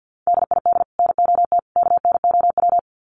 Morse code, heard as a keyed tone, about 50 wpm, 700 Hz. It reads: DID NOT KNOW
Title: Sets of 3 Words 50wpm